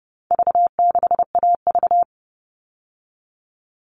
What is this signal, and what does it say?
Morse code, heard as a keyed tone, about 30 wpm, 700 Hz. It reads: V6AV
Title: All Cut Numbers - 4-Digit 30wpm